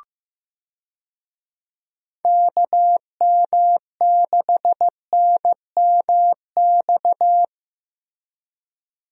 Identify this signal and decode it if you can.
Morse code, heard as a keyed tone, about 15 wpm, 700 Hz. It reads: KM6NMX
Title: USA Call Signs - 6 Characters - Repeated 2x - 15wpm